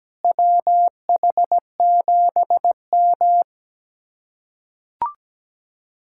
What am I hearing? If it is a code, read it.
Morse code, heard as a keyed tone, about 17 wpm, 700 Hz. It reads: WH7M
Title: USA Callsign DE Callsign - 4 Characters - 17wpm